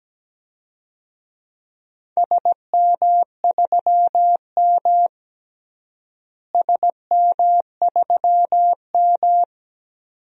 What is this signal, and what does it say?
Morse code, heard as a keyed tone, about 17 wpm, 700 Hz. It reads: SM3M SM3M
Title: DXCC-100 Call Signs - 4 Characters - Repeated 2x - 17wpm